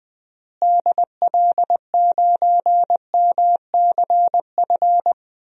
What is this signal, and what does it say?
Morse code, heard as a keyed tone, about 21 wpm, 700 Hz. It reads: DL9MCF
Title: DXCC-100 Call Signs - 6 Characters - Repeated 3x - 20wpm